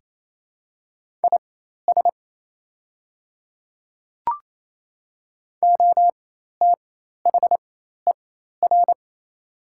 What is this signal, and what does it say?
Morse code, heard as a keyed tone, about 28 wpm, spaced as for 13 wpm, 700 Hz. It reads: IS OTHER
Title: 4x Character Spacing - Top 100 Words - 28wpm